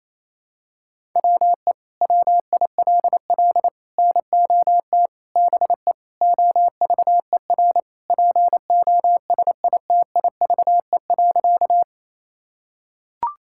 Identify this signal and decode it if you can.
Morse code, heard as a keyed tone, about 28 wpm, 700 Hz. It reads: WE WILL NOT BE OVER POSITIVE.
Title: Sentences from Top 1000 Words - Review Part 5 - 28wpm